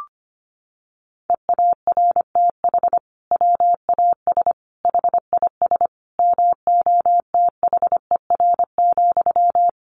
Morse code, heard as a keyed tone, about 25 wpm, 700 Hz. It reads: EARTH WAS HIS MOTHER,
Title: Sentences from Top 1200 Words 25wpm